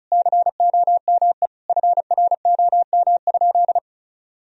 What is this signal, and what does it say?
Morse code, heard as a keyed tone, about 36 wpm, 700 Hz. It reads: COME FROM?
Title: Sentences from Top 1000 Words - Review Part 2 - 35wpm